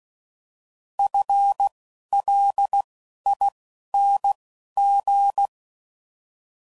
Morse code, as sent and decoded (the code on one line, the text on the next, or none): ..-. / .-.. / .. / -. / --.
F L I N G